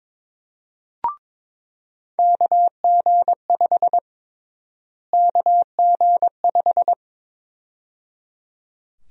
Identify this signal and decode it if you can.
Morse code, heard as a keyed tone, about 22 wpm, 700 Hz. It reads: KG5 KG5
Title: USA Call Sign Prefixes - Repeated 2x - 22wpm